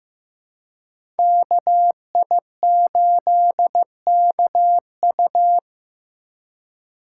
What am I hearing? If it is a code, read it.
Morse code, heard as a keyed tone, about 15 wpm, 700 Hz. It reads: KI8KU
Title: USA Call Signs - 5 Characters - 4x Character Spacing - 15wpm